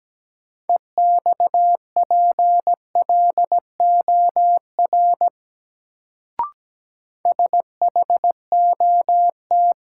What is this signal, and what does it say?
Morse code, heard as a keyed tone, about 17 wpm, 700 Hz. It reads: EXPLOR SHOT
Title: Sentences from Top 1100 Words - New Word Drill Alt - 17wpm